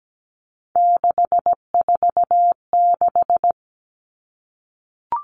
Morse code, heard as a keyed tone, about 17 wpm, 700 Hz. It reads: 646
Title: Numbers - 3-Digit 17wpm